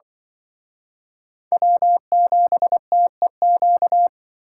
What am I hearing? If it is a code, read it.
Morse code, heard as a keyed tone, about 25 wpm, 700 Hz. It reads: W7TEQ
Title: Speed Racing - USA 1x3 Call Signs 20wpm